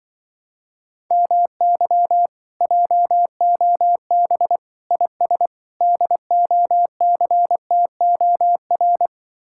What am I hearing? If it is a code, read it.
Morse code, heard as a keyed tone, about 25 wpm, 700 Hz. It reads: MY JOB IS DOCTOR